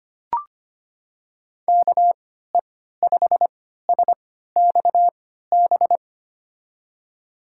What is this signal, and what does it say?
Morse code, heard as a keyed tone, about 25 wpm, spaced as for 14 wpm, 700 Hz. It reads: KE5SXB
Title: USA Call Signs - 6 Characters - 3x Character Spacing - 25wpm